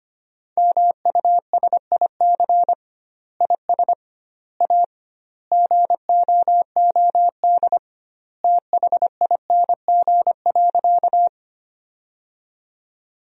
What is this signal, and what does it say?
Morse code, heard as a keyed tone, about 25 wpm, 700 Hz. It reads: MUSIC IS A GOOD THING.
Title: 2x Word-Spacing - Sentences from Top 300 Words - 25wpm